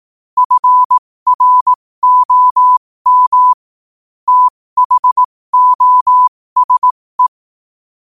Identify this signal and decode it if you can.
Morse code, heard as a keyed tone, about 18 wpm, spaced as for 15 wpm, 1000 Hz. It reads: FROM THOSE
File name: qotdfiles / 15wpm